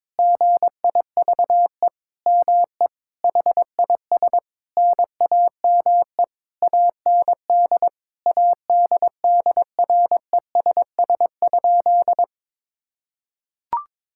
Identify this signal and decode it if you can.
Morse code, heard as a keyed tone, about 21 wpm, 700 Hz. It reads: GIVE ME HIS NAME AND ADDRESS?
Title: Sentences from Top 1000 Words - Review Part 6 - 22wpm